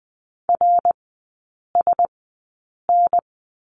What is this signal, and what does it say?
Morse code, heard as a keyed tone, about 20 wpm, spaced as for 8 wpm, 700 Hz. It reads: RSN
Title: 053 Lesson 10e - Letter r - Three Characters 20wpm